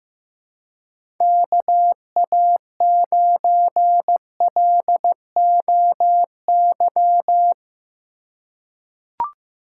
Morse code, heard as a keyed tone, about 15 wpm, 700 Hz. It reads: KA9LOY
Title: USA Call Signs - 6 Characters - 4x Character Spacing - 15wpm